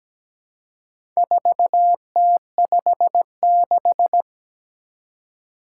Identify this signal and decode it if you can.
Morse code, heard as a keyed tone, about 17 wpm, 700 Hz. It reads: 4T56
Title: Four Challenging Letter-Number Combinations 17wpm